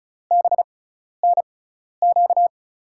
Morse code, heard as a keyed tone, about 36 wpm, spaced as for 12 wpm, 700 Hz. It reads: BNQ
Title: USA Call Signs - 6x Character Spacing - 35wpm